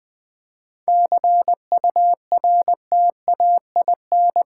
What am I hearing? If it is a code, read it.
Morse code, heard as a keyed tone, about 20 wpm, 700 Hz. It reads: CURTAIN
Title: 7 Letter Words - First 3 Letters Given - 20wpm